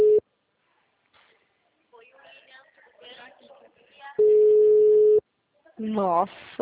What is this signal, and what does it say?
Ringback tone (Europe, 425 Hz), with speech.
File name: out- -2024- - - .68